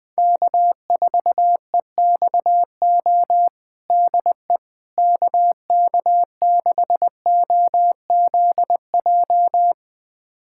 Morse code, heard as a keyed tone, about 20 wpm, 700 Hz. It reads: K4EXO DE KK6OZJ
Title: USA Callsign DE Callsign 20wpm